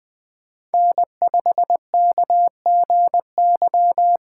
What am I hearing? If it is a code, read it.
Morse code, heard as a keyed tone, about 20 wpm, 700 Hz. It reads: N5KGY
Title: 2023 POTA Hunter Call Signs - 20wpm